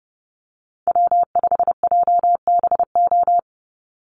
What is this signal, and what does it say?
Morse code, heard as a keyed tone, about 30 wpm, 700 Hz. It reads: W5JBO
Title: Call Signs Repeated 2x - 30wpm